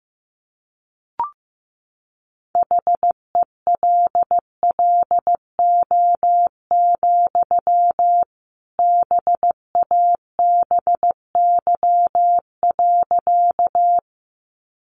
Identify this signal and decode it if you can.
Morse code, heard as a keyed tone, about 15 wpm, 700 Hz. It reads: HELLO, BABY.
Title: Sentences from Top 1300 Words 15wpm